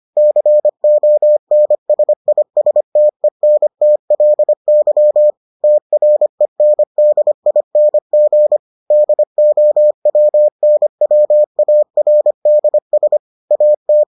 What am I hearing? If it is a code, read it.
Morse code, heard as a keyed tone, about 25 wpm, 600 Hz. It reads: CONSISTENTLY TRENDING DOWNWARDS AT